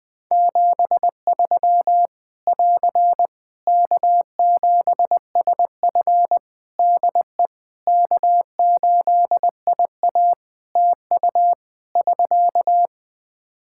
Morse code, heard as a keyed tone, about 21 wpm, 700 Hz. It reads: 73 <AR> K7SF DE K8IA TU <SK>
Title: Practice QSO - Exchange 4 - Ending - 20wpm